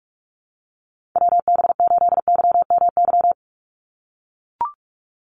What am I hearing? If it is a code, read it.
Morse code, heard as a keyed tone, about 44 wpm, 700 Hz. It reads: WB8YMY
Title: USA Call Signs 45wpm